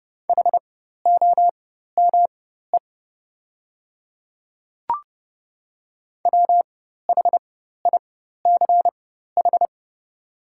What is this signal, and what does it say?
Morse code, heard as a keyed tone, about 30 wpm, spaced as for 14 wpm, 700 Hz. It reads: HOME WHICH
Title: 4x Character Spacing - Top 500 Words - 30wpm